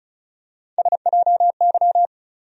Morse code, heard as a keyed tone, about 36 wpm, 700 Hz. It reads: SJY
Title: USA Call Sign Postfixes - 3 Letters 35wpm